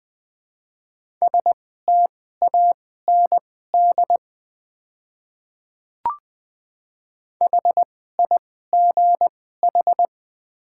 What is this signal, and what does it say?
Morse code, heard as a keyed tone, about 20 wpm, spaced as for 15 wpm, 700 Hz. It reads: STAND HIGH
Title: 2x Character Spacing - Top 300 Words - 20wpm